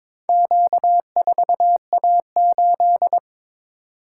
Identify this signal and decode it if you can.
Morse code, heard as a keyed tone, about 22 wpm, 700 Hz. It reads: Q4A8
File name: practice / 22wpm / Four Letter-Number Combinations - 22wpm